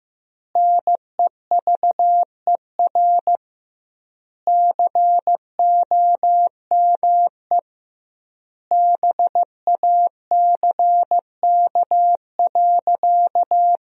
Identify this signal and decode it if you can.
Morse code, heard as a keyed tone, about 16 wpm, 700 Hz. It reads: NEVER COME BACK.
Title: 2x Word-Spacing - Sentences from Top 300 Words - 15wpm